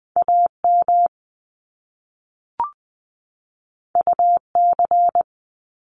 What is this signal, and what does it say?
Morse code, heard as a keyed tone, about 20 wpm, 700 Hz. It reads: AM UC
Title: Top Two Letter Combinations in Words 20wpm